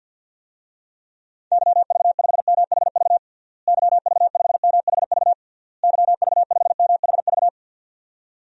Morse code, heard as a keyed tone, about 52 wpm, 700 Hz. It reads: YV5MHV YV5MHV YV5MHV
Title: DXCC-100 Call Signs - 6 Characters - Repeated 3x - 50wpm